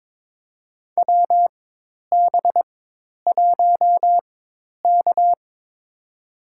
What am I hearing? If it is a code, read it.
Morse code, heard as a keyed tone, about 22 wpm, spaced as for 10 wpm, 700 Hz. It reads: WB1K
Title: USA Call Signs - 4 Characters - 4x Character Spacing - 22wpm